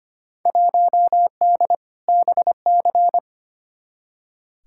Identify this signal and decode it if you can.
Morse code, heard as a keyed tone, about 25 wpm, 700 Hz. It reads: 1D BC
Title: Field Day Exchanges - 25wpm